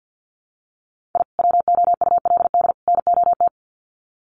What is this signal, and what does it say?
Morse code, heard as a keyed tone, about 50 wpm, 700 Hz. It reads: I WOULD NOT
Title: Sets of 3 Words 50wpm